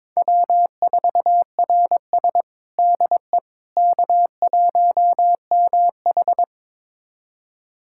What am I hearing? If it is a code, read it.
Morse code, heard as a keyed tone, about 22 wpm, 700 Hz. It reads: W4RS DE K1MH
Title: USA Callsign DE Callsign - 4 Characters - 22wpm